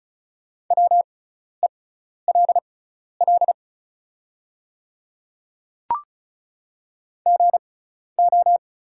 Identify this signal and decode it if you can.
Morse code, heard as a keyed tone, about 36 wpm, spaced as for 12 wpm, 690 Hz. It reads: WELL GO
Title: 6x Character Spacing - Top 200 Words - 35wpm